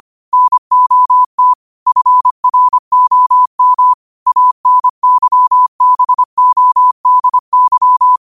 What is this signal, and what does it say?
Morse code, heard as a keyed tone, about 25 wpm, 1000 Hz. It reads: NOT FROM ANYBODY
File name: qotdfiles / 25wpm